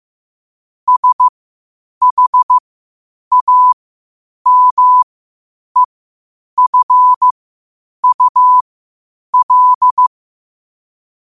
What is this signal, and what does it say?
Morse code, heard as a keyed tone, about 15 wpm, spaced as for 8 wpm, 1000 Hz. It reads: SHAMEFUL
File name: qotdfiles / 8wpm